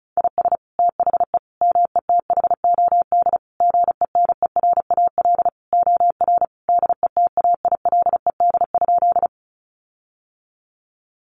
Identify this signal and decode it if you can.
Morse code, heard as a keyed tone, about 34 wpm, 700 Hz. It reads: IS THE METHOD GENERAL OR DETAILED?
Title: Sentences from Top 1500 Words 35wpm